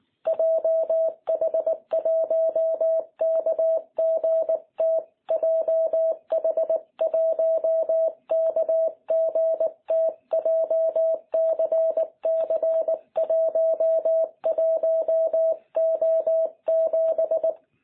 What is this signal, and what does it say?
Morse code, heard as a keyed tone, about 19 wpm, 650 Hz. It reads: JH1KGT JH1KGT JCC11O7
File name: jh8cbh / 2021cw